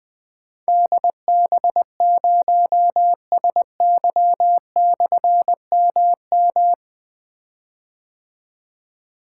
Morse code, heard as a keyed tone, about 20 wpm, 700 Hz. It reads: DB0SY/MM
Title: Difficult Call Signs 20wpm